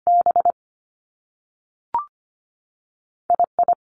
Morse code, heard as a keyed tone, about 25 wpm, spaced as for 3 wpm, 700 Hz. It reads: BI I
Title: Top Two Letter Combinations in Words 25wpm